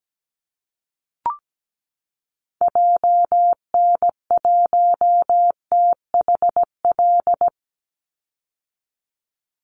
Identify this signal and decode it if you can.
Morse code, heard as a keyed tone, about 17 wpm, 700 Hz. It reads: JN1THL
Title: Call Signs - 17wpm